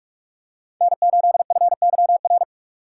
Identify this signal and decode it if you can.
Morse code, heard as a keyed tone, about 44 wpm, 700 Hz. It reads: N8FYR